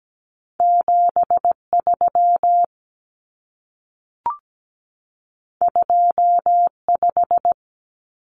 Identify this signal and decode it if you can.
Morse code, heard as a keyed tone, about 17 wpm, 700 Hz. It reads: 73 25
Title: Numbers - 2-Digit 17wpm